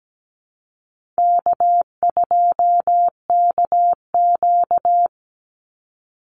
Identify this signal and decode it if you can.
Morse code, heard as a keyed tone, about 17 wpm, 700 Hz. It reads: K2KQ
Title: Call Signs Repeated 3x - 17wpm